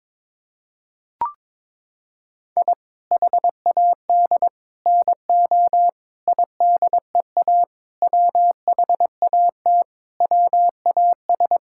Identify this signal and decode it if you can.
Morse code, heard as a keyed tone, about 22 wpm, 700 Hz. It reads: I HAD NO IDEA WHAT WAS
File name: practice / 22wpm / Sets of 8 Words - 22wpm